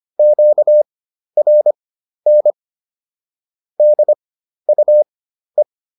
Morse code, heard as a keyed tone, about 25 wpm, spaced as for 12 wpm, 600 Hz. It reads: QRN DUE
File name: SS017 12